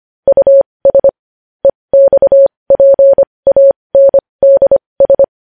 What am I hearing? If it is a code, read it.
Morse code, heard as a keyed tone, about 25 wpm, spaced as for 20 wpm, 550 Hz. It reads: US EXPANDS
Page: news-headlines-in-morse-code-at-20-wpm